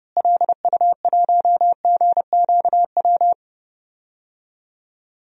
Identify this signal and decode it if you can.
Morse code, heard as a keyed tone, about 31 wpm, 700 Hz. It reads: LU1GQW
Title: DXCC-100 Call Signs - 6 Characters - Repeated 2x - 30wpm